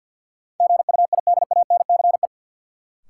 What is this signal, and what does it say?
Morse code, heard as a keyed tone, about 52 wpm, 700 Hz. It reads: GUIDANCE
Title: Sentences from Top 1600 Words - New Word Drill Alt - 50wpm